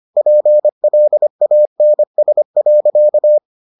Morse code, heard as a keyed tone, about 25 wpm, 600 Hz. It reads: PLANS.